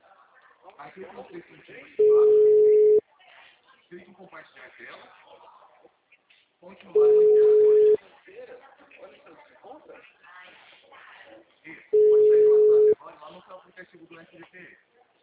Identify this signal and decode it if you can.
Ringback tone (Europe, 425 Hz), with speech.